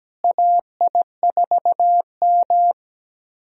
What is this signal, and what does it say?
Morse code, heard as a keyed tone, about 17 wpm, 700 Hz. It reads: AI4M
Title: USA Call Signs - 4 Characters - Repeated 3x - 17wpm